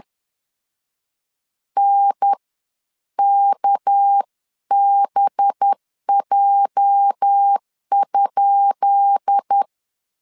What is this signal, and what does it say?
Morse code, heard as a keyed tone, about 10 wpm, 800 Hz. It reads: N KBJ?
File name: LEZIONI / 10 SIGNS / 10B INT 10wpm 2844